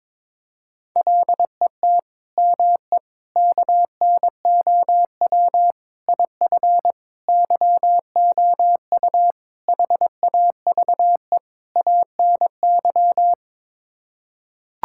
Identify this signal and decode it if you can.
Morse code, heard as a keyed tone, about 22 wpm, 700 Hz. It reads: LET ME KNOW IF YOU HAVE ANY
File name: practice / 22wpm / Sets of 7 Words - 22wpm